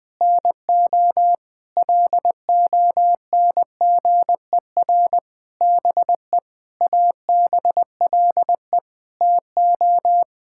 Morse code, heard as a keyed tone, about 20 wpm, 700 Hz. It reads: NO LONGER BE ABLE TO
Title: Sentences with 6-Word Phrases - 20wpm